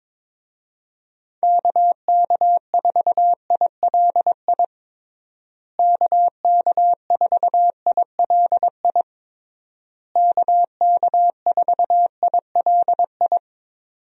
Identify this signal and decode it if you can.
Morse code, heard as a keyed tone, about 22 wpm, 700 Hz. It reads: KK4ILI KK4ILI KK4ILI
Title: USA Call Signs - 6 Characters - Repeated 3x - 22wpm